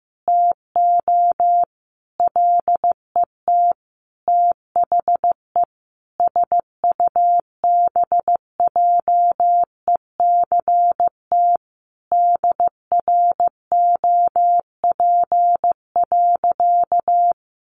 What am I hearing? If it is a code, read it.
Morse code, heard as a keyed tone, about 15 wpm, 700 Hz. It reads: TO LET THE SUBJECT DROP.
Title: Sentences from Top 1400 Words 15wpm